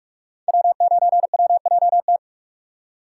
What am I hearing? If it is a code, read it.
Morse code, heard as a keyed tone, about 46 wpm, 700 Hz. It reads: W9WJT
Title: Code Groups - 5 Letter:Numbers - 45wpm